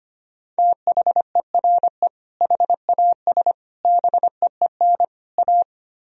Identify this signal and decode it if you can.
Morse code, heard as a keyed tone, about 32 wpm, 700 Hz. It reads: THERE HAS BEEN A